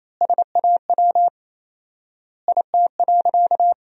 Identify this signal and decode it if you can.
Morse code, heard as a keyed tone, about 28 wpm, 700 Hz. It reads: SAW IT.
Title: 4x Word-Spacing - Sentences from Top 300 Words - 28wpm